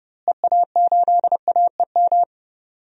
Morse code, heard as a keyed tone, about 30 wpm, 700 Hz. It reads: EA8AEM
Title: DXCC-100 Call Signs - 6 Characters - Repeated 3x - 30wpm